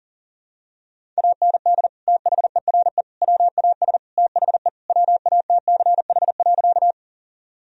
Morse code, heard as a keyed tone, about 40 wpm, 700 Hz. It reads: AND THERE WAS THE WATCH.
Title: Sentences from Top 1000 Words - Review Part 1 - 40wpm